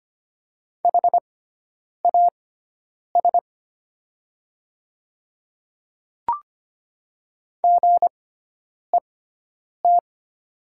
Morse code, heard as a keyed tone, about 25 wpm, spaced as for 9 wpm, 690 Hz. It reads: HAS GET
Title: 6x Character Spacing - Top 200 Words - 25wpm